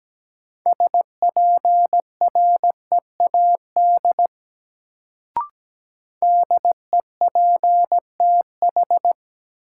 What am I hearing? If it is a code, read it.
Morse code, heard as a keyed tone, about 17 wpm, 700 Hz. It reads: SPREAD DEPTH
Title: Sentences from Top 1200 Words - New Word Drill - 17wpm